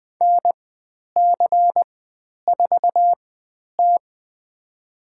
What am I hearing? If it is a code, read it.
Morse code, heard as a keyed tone, about 20 wpm, spaced as for 10 wpm, 700 Hz. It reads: NC4T
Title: 095 Lesson 16g - Letter c - Call Signs 20wpm